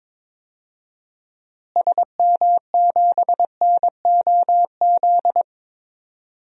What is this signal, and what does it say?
Morse code, heard as a keyed tone, about 22 wpm, 700 Hz. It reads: SM7NOZ
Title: DXCC-100 Call Signs - 6 Characters - Repeated 3x - 22wpm